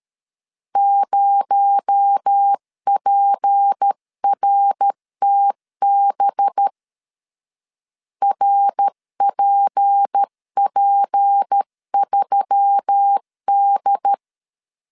Morse code, heard as a keyed tone, about 13 wpm, 800 Hz. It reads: 0PRTB RPP3D
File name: CW / DET / 1 6 1500C S C 12wpm 3533